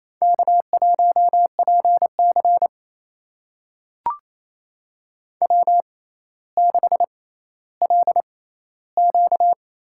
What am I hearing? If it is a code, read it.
Morse code, heard as a keyed tone, about 28 wpm, spaced as for 14 wpm, 700 Hz. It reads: K1PC W6LQ
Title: USA Call Signs - 6x Character Spacing - 28wpm